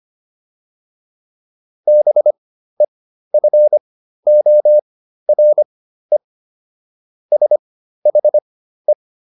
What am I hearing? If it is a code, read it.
Morse code, heard as a keyed tone, about 25 wpm, spaced as for 13 wpm, 600 Hz. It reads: BEFORE SHE